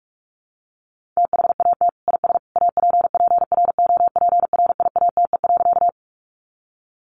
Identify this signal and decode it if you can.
Morse code, heard as a keyed tone, about 44 wpm, 700 Hz. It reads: THAT IS APPROPRIATE.
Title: Sentences from Top 900 Words 45wpm